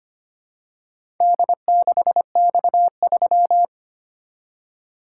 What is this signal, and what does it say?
Morse code, heard as a keyed tone, about 25 wpm, 700 Hz. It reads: D6X3
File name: practice / 25wpm / Four Letter-Number Combinations - 25wpm